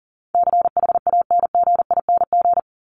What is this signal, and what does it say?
Morse code, heard as a keyed tone, about 37 wpm, 700 Hz. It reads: CHANGING
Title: ING Suffix - Top 100 Words with Base Words 40wpm